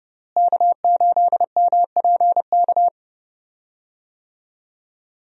Morse code, heard as a keyed tone, about 30 wpm, 700 Hz. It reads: K8MPK
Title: USA Call Signs - 5 Characters - Repeated 2x - 30wpm